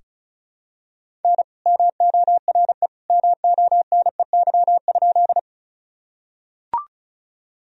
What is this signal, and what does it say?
Morse code, heard as a keyed tone, about 36 wpm, 700 Hz. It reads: N MORE MONEY?
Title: Sentences from Top 400 Words and Abbreviations 35wpm